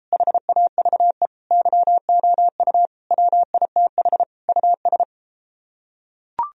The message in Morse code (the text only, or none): HAVE YOU WITH US